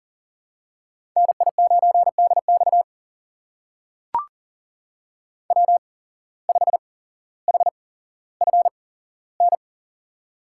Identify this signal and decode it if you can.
Morse code, heard as a keyed tone, about 39 wpm, spaced as for 17 wpm, 700 Hz. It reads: NI9DX W5HFN
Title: USA Call Signs - 5 Characters - 8x Character Spacing - 40wpm